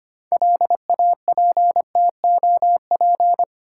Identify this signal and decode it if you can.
Morse code, heard as a keyed tone, about 25 wpm, 690 Hz. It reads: LAPTOP
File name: Easy Words - Speed-Racing with Hints - 25wpm